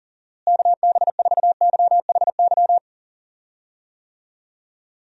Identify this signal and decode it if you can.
Morse code, heard as a keyed tone, about 38 wpm, 700 Hz. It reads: KB4YHY
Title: USA Call Signs - 6 Characters - Repeated 3x - 40wpm